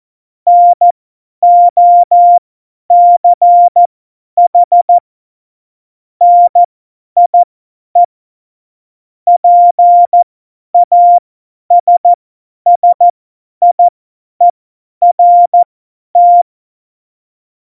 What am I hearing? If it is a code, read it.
Morse code, heard as a keyed tone, about 14 wpm, spaced as for 10 wpm, 700 Hz. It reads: NOCH NIE PASSIERT
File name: Morsekurs / Witze / Witz 29. Dezember 2020, 50 F70